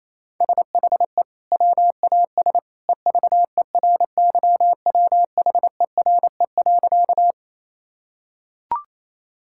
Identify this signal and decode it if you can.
Morse code, heard as a keyed tone, about 28 wpm, 700 Hz. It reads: SHE WAS EVERYWHERE.
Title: Sentences from Top 1600 Words 28wpm